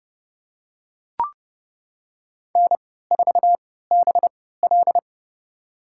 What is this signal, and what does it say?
Morse code, heard as a keyed tone, about 30 wpm, spaced as for 17 wpm, 700 Hz. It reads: N4BL
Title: USA Call Signs - 4 Characters - 3x Character Spacing - 30wpm